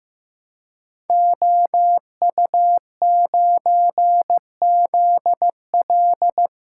Morse code, heard as a keyed tone, about 15 wpm, 700 Hz. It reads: OU9ZL
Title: Code Groups - 5 Letter:Numbers - 15wpm